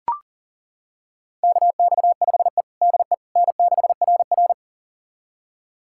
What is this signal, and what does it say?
Morse code, heard as a keyed tone, about 40 wpm, 700 Hz. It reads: KX5E DE N6RR
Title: USA Callsign DE Callsign - 4 Characters - 40wpm